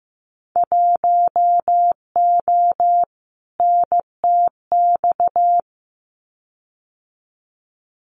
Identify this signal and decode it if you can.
Morse code, heard as a keyed tone, about 15 wpm, 700 Hz. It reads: 1O NTX
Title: Winter Field Day Exchanges 15wpm